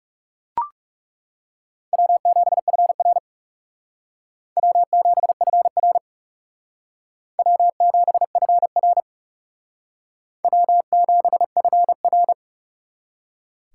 Morse code, heard as a keyed tone, about 37 wpm, 700 Hz. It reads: W7FR W7FR W7FR W7FR
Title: Speed Racing - USA Call Signs 30wpm